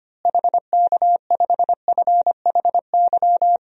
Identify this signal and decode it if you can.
Morse code, heard as a keyed tone, about 25 wpm, 700 Hz. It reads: HK5FHY
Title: DXCC-100 Call Signs - 6 Characters - Repeated 2x - 25wpm